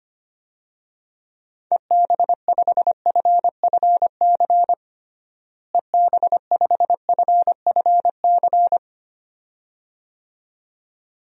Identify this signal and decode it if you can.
Morse code, heard as a keyed tone, about 25 wpm, 700 Hz. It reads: EB5FFC EB5FFC
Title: DXCC-100 Call Signs - 6 Characters - Repeated 2x - 25wpm